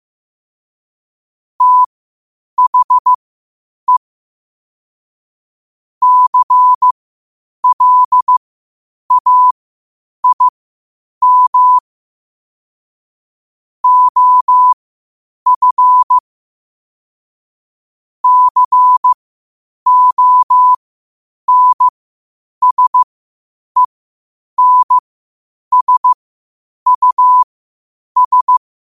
Morse code, heard as a keyed tone, about 15 wpm, spaced as for 8 wpm, 1000 Hz. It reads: THE CLAIM OF CONSENSUS
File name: qotdfiles / 8wpm